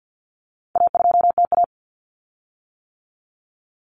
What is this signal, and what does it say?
Morse code, heard as a keyed tone, about 50 wpm, 700 Hz. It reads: A2TA
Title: Common Cut Numbers - 4-Digit 50wpm